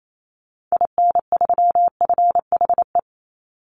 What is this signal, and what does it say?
Morse code, heard as a keyed tone, about 28 wpm, 700 Hz. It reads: IN3FHE
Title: Call Signs Repeated 2x - 28wpm